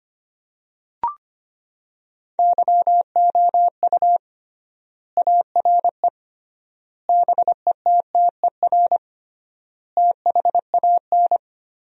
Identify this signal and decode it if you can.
Morse code, heard as a keyed tone, about 25 wpm, 700 Hz. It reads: YOU ARE BETTER THAN
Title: 3x Word-Spacing - Sentences from Top 400 Words - 25wpm